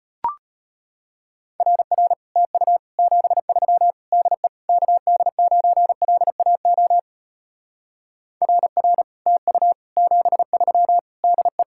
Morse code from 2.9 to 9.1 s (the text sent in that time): 73 DE KD9LAO RR